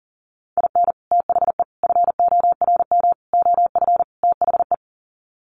Morse code, heard as a keyed tone, about 40 wpm, 700 Hz. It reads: IN THE FORM OF THE
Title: Sets of 5 Words Encore2 40wpm